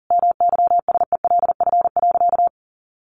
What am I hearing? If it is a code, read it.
Morse code, heard as a keyed tone, about 40 wpm, 700 Hz. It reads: MYSELF.
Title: Sentences from Top 1300 Words 40wpm